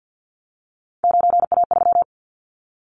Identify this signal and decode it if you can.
Morse code, heard as a keyed tone, about 50 wpm, 700 Hz. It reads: 9A3
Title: Callsign Prefixes for Top 25 DXCC - 50wpm